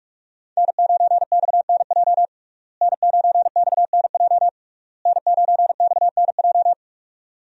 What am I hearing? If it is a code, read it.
Morse code, heard as a keyed tone, about 45 wpm, 700 Hz. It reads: N9XNJ N9XNJ N9XNJ
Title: USA Call Signs - 5 Characters - Repeated 3x - 45wpm